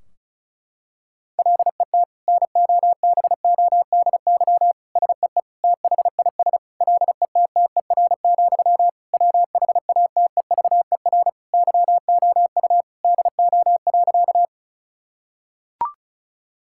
Morse code, heard as a keyed tone, about 36 wpm, 700 Hz. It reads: LET NOBODY SEE THIS LETTER, WHATEVER YOU DO.
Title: Sentences from Top 1600 Words 35wpm